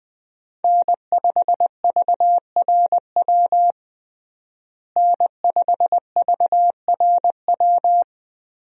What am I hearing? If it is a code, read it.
Morse code, heard as a keyed tone, about 21 wpm, 700 Hz. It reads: N5VRW N5VRW
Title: USA Call Signs - 5 Characters - Repeated 3x - 20wpm